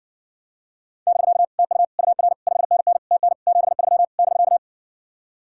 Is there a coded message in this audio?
Yes: Morse code from 1.1 to 4.6 s, heard as a keyed tone, about 60 wpm, 700 Hz. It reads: <BK> TU UR 5NN TN 73 <BK>